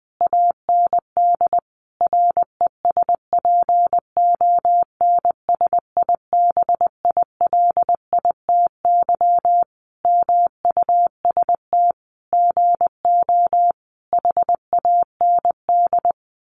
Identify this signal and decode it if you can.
Morse code, heard as a keyed tone, about 20 wpm, 700 Hz. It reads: AND RESPONSIBILITY MUST GO HAND
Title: Sentences from Top 1200 Words 20wpm